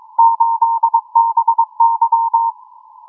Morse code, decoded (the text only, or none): 8BY